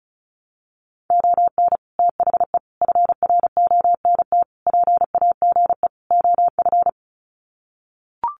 Morse code, heard as a keyed tone, about 34 wpm, 700 Hz. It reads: ON THE FRONT PAGE OF
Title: Sets of 5 Words Encore 35wpm